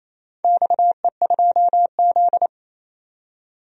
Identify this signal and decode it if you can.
Morse code, heard as a keyed tone, about 28 wpm, 700 Hz. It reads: XE2Z
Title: DXCC-100 Call Signs - 4 Characters - 28wpm